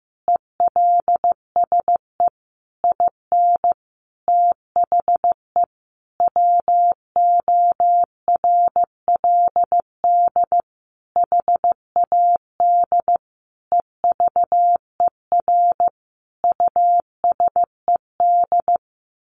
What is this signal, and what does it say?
Morse code, heard as a keyed tone, about 15 wpm, 700 Hz. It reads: ELSE IN THE WORLD HAD EVER USED
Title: Sentences from Top 1100 Words 15wpm